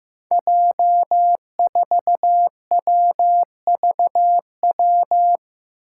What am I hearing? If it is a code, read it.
Morse code, heard as a keyed tone, about 15 wpm, 700 Hz. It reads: J4WVW
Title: Five Challenging Letter-Number Combinations 15wpm